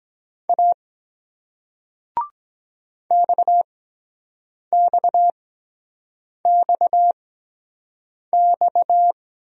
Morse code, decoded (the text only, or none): A XXXX